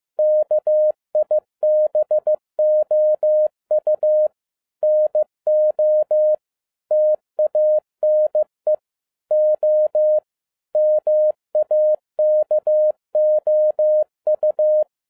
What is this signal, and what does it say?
Morse code, heard as a keyed tone, about 15 wpm, 610 Hz. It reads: KIBOU NO TANE O MAKOU
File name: 2012-01-mp3 / n13-A NEWLY DESIGNED QSL CARD-15wpm